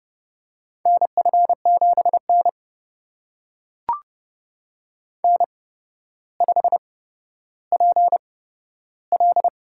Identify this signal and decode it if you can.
Morse code, heard as a keyed tone, about 30 wpm, spaced as for 13 wpm, 700 Hz. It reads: NF7N N5PL
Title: USA Call Signs - 4 Characters - 8x Character Spacing - 30wpm